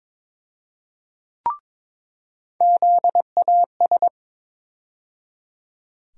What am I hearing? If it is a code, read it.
Morse code, heard as a keyed tone, about 22 wpm, 700 Hz. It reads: ZAS
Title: USA Call Sign Postfixes - 3 Letters 22wpm